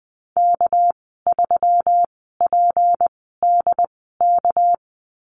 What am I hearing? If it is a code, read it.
Morse code, heard as a keyed tone, about 20 wpm, spaced as for 15 wpm, 700 Hz. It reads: K3PDK
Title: 200 Lesson 32g - Letter k - Call Signs 20wpm